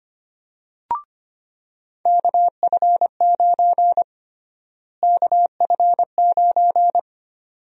Morse code, heard as a keyed tone, about 25 wpm, 700 Hz. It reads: KF9 KF9
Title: USA Call Sign Prefixes - Repeated 2x - 25wpm